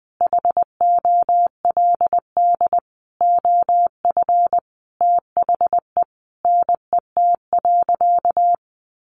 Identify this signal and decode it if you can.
Morse code, heard as a keyed tone, about 20 wpm, 700 Hz. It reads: HOLD OF THE NET.
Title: Sentences from Top 1500 Words 20wpm